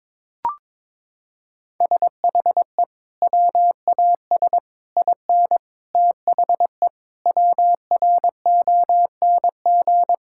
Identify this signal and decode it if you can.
Morse code, heard as a keyed tone, about 22 wpm, 700 Hz. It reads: SHE WAS IN THE WRONG
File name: Sentences with 9-Word Phrases - 22wpm